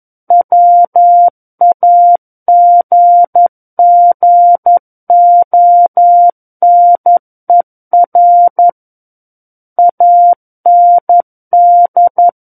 Morse code, heard as a keyed tone, about 11 wpm, 700 Hz. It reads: WAGGONER AND
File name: Book One / 02-Chapter 2 - 11 WPM